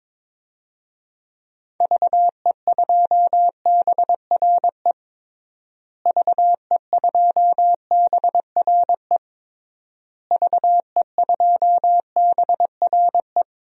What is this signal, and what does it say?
Morse code, heard as a keyed tone, about 22 wpm, 700 Hz. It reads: VE2BRE VE2BRE VE2BRE
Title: DXCC-100 Call Signs - 6 Characters - Repeated 3x - 22wpm